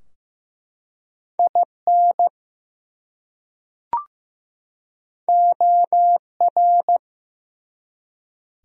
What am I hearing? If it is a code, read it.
Morse code, heard as a keyed tone, about 15 wpm, 700 Hz. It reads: IN OR
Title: Top 100 Words 15wpm